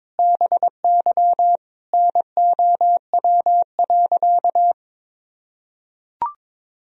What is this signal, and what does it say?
Morse code, heard as a keyed tone, about 21 wpm, 700 Hz. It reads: BY NOW.
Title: Sentences from Top 1000 Words - Review Part 3 - 22wpm